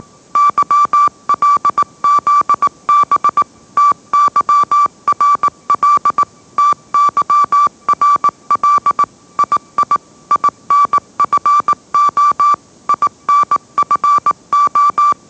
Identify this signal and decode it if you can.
Morse code, heard as a keyed tone, about 18 wpm, 1160 Hz. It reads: YLZB TYRL TYRL II INFO INFO
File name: mrx / mrx082 20wpm